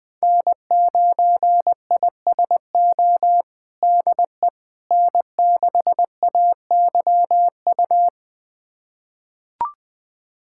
Morse code, heard as a keyed tone, about 20 wpm, 700 Hz. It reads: N9ISO DE N6AYU
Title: USA Callsign DE Callsign - 5 Characters - 20wpm